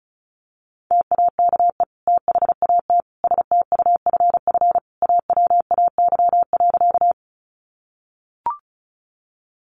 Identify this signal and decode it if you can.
Morse code, heard as a keyed tone, about 34 wpm, 700 Hz. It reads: TAKE THAT STUFF AWAY.
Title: Sentences from Top 1400 Words 35wpm